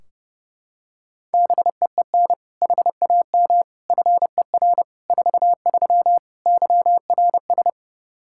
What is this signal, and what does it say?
Morse code, heard as a keyed tone, about 30 wpm, 700 Hz. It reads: BEEN HAM FER 43 YRS
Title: Practice QSO - Exchange 3 - More Info - 30wpm